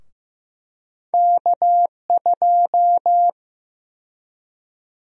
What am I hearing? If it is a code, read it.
Morse code, heard as a keyed tone, about 15 wpm, 700 Hz. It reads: K2
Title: USA Call Sign Prefixes - Repeated 2x - 15wpm